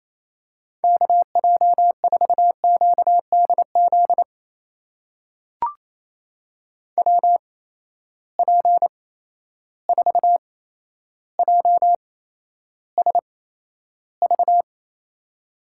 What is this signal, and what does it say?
Morse code, heard as a keyed tone, about 27 wpm, spaced as for 12 wpm, 700 Hz. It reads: KJ4QDZ WP4JSV
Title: USA Call Signs - 6 Characters - 8x Character Spacing - 28wpm